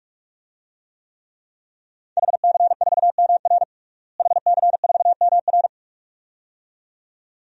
Morse code, heard as a keyed tone, about 46 wpm, 700 Hz. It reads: HC4MR HC4MR
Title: DXCC-100 Call Signs - 5 Characters - Repeated 2x - 45wpm